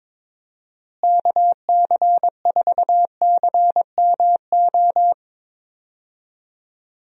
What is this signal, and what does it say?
Morse code, heard as a keyed tone, about 22 wpm, 700 Hz. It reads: KC4CMO
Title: USA Call Signs - 6 Characters - Repeated 3x - 22wpm